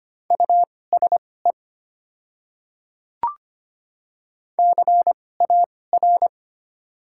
Morse code, heard as a keyed tone, about 25 wpm, spaced as for 18 wpm, 690 Hz. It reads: USE CAR
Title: 2x Character Spacing - Top 300 Words - 25wpm